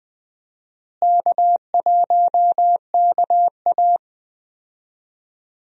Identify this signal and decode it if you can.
Morse code, heard as a keyed tone, about 20 wpm, 700 Hz. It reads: K1KA
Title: USA Call Signs - 4 Characters - Repeated 2x - 20wpm